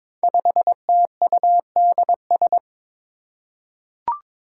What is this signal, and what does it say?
Morse code, heard as a keyed tone, about 22 wpm, 700 Hz. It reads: 5TUDS
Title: Five Challenging Letter-Number Combinations 22wpm